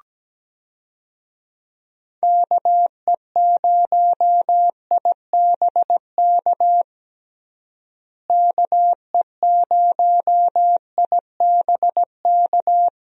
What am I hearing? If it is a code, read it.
Morse code, heard as a keyed tone, about 17 wpm, 700 Hz. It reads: KE0IBK KE0IBK
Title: USA Call Signs - 6 Characters - Repeated 2x - 17wpm